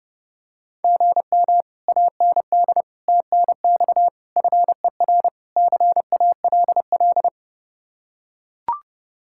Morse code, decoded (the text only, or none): GM AND TNX FER CALL